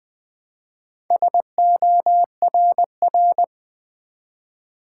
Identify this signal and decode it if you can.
Morse code, heard as a keyed tone, about 20 wpm, 700 Hz. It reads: SORR
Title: Build A Word - 1 Letter At A Time - 3 to 5-Letter Words - 20wpm